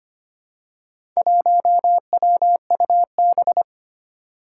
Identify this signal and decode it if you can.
Morse code, heard as a keyed tone, about 25 wpm, 700 Hz. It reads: 1WUB
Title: Four Challenging Letter-Number Combinations 25wpm